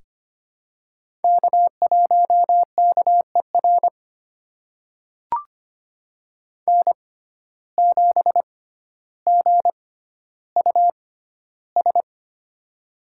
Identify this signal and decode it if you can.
Morse code, heard as a keyed tone, about 25 wpm, spaced as for 13 wpm, 700 Hz. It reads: K1KER N7GUS
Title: USA Call Signs - 5 Characters - 6x Character Spacing - 25wpm